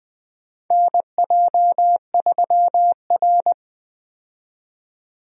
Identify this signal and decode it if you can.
Morse code, heard as a keyed tone, about 20 wpm, 700 Hz. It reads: NJ3R
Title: USA Call Signs - 4 Characters - 20wpm